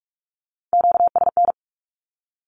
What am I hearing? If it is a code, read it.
Morse code, heard as a keyed tone, about 46 wpm, 700 Hz. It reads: QSN
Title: Q-Codes with Spoken Meaning - 45wpm